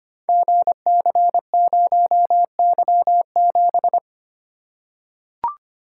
Morse code, heard as a keyed tone, about 25 wpm, 690 Hz. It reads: GC0Y7
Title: Code Groups - 5 Letter:Numbers - 25wpm